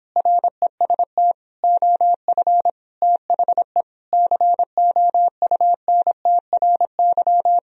Morse code, heard as a keyed tone, about 29 wpm, 700 Hz. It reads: REST OF THE COUNTRY